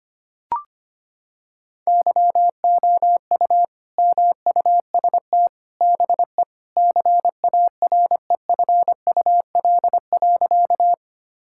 Morse code, heard as a keyed tone, about 25 wpm, 700 Hz. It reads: YOU MUST BE CAREFUL.
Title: Sentences from Top 1600 Words 25wpm